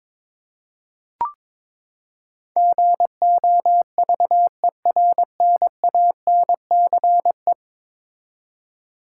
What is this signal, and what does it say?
Morse code, heard as a keyed tone, about 21 wpm, 700 Hz. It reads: GOVERNANCE
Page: morse-code-ninja-practice-22wpm